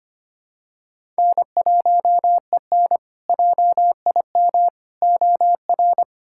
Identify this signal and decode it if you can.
Morse code, heard as a keyed tone, about 25 wpm, 700 Hz. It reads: N1EN JIM OR
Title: K1USN SST QSO Practice - 25wpm